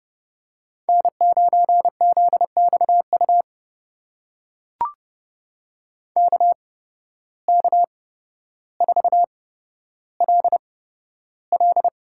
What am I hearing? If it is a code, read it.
Morse code, heard as a keyed tone, about 30 wpm, spaced as for 13 wpm, 700 Hz. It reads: N9ZXU KK4LL
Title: USA Call Signs - 8x Character Spacing - 30wpm